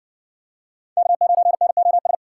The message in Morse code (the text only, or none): K9NOS